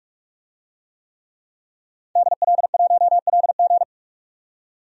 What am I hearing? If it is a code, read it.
Morse code, heard as a keyed tone, about 46 wpm, 700 Hz. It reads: DL1LG
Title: DXCC-100 Call Signs - 5 Characters - 45wpm